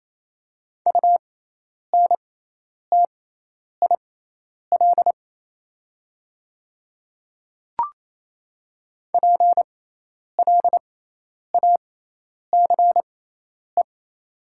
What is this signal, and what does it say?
Morse code, heard as a keyed tone, about 28 wpm, spaced as for 10 wpm, 700 Hz. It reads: UNTIL PLACE
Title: 6x Character Spacing - Top 500 Words - 28wpm